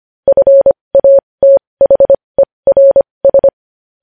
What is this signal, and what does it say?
Morse code, heard as a keyed tone, about 25 wpm, spaced as for 20 wpm, 550 Hz. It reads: FATHERS
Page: news-headlines-in-morse-code-at-20-wpm